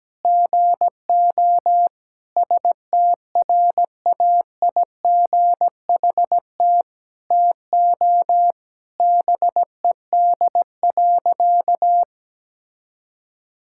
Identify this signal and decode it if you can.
Morse code, heard as a keyed tone, about 17 wpm, 700 Hz. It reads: GO STRAIGHT TO BED.
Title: Sentences from Top 1000 Words - Review Part 6 - 17wpm